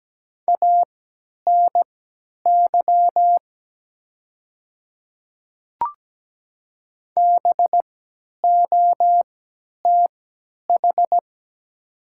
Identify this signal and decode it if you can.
Morse code, heard as a keyed tone, about 17 wpm, spaced as for 10 wpm, 700 Hz. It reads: ANY BOTH
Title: 3x Character Spacing - Top 300 Words - 17wpm